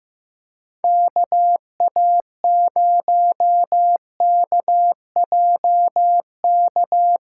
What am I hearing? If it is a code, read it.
Morse code, heard as a keyed tone, about 15 wpm, 700 Hz. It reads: KA0KJK
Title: USA Call Signs - 6 Characters - 8x Character Spacing - 15wpm